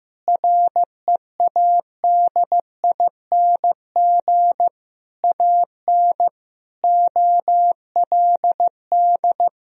Morse code, heard as a keyed tone, about 15 wpm, 700 Hz. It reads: READING AN OLD
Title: Sentences from Top 1000 Words - Review Part 4 - 15wpm